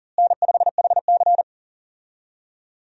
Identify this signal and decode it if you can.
Morse code, heard as a keyed tone, about 40 wpm, 700 Hz. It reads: N5HC
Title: USA Call Signs - 4 Characters - Repeated 3x - 40wpm